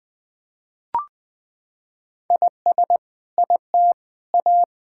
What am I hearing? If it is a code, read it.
Morse code, heard as a keyed tone, about 21 wpm, 700 Hz. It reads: IS IT A
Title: Sentences from Top 1600 Words 20wpm